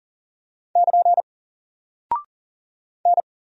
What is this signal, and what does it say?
Morse code, heard as a keyed tone, about 40 wpm, 700 Hz. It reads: <KN> N
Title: Single Character (Letter:Number:Punctuation:Prosign) 40wpm